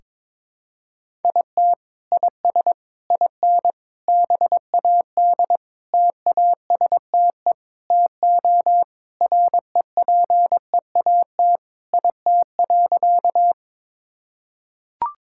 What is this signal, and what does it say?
Morse code, heard as a keyed tone, about 22 wpm, 700 Hz. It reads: IT IS IN BAD TASTE TO REPEAT IT.
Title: Sentences from Top 1600 Words 22wpm